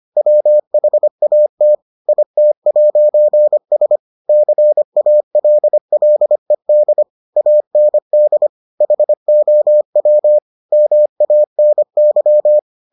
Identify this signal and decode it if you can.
Morse code, heard as a keyed tone, about 25 wpm, 600 Hz. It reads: WHAT IT'S CALLED AND HOW MANY